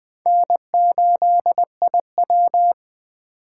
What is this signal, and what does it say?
Morse code, heard as a keyed tone, about 21 wpm, 700 Hz. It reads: N8IW
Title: USA Call Signs - 8x Character Spacing - 20wpm